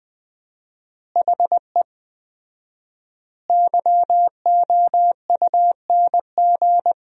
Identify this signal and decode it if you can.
Morse code, heard as a keyed tone, about 21 wpm, 700 Hz. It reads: HE YOUNG
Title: 4x Word-Spacing - Sentences from Top 400 Words - 20wpm